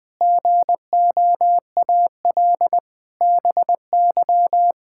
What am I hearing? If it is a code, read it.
Morse code, heard as a keyed tone, about 20 wpm, 700 Hz. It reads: GOAL BY
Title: Sentences from Top 1000 Words - Review Part 5 - 20wpm